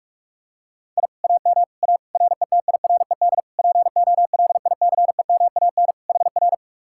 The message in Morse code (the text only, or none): I AM A RETIRED POLICEMAN HR